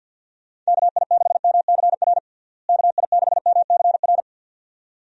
Morse code, heard as a keyed tone, about 46 wpm, 700 Hz. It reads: KI6MCR KI6MCR
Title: USA Call Signs - 6 Characters - Repeated 2x - 50wpm